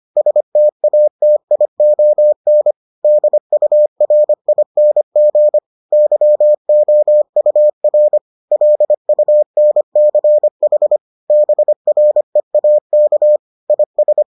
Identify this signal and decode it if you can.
Morse code, heard as a keyed tone, about 25 wpm, 600 Hz. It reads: STATION DURING YOUR LUNCH BREAK IS